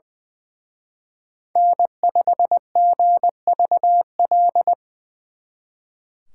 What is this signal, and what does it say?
Morse code, heard as a keyed tone, about 24 wpm, 700 Hz. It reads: N5GVL
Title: Speed Racing - USA 1x3 Call Signs 20wpm